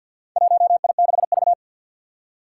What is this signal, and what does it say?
Morse code, heard as a keyed tone, about 54 wpm, 700 Hz. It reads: 1I6V
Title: Four Challenging Letter-Number Combinations 50wpm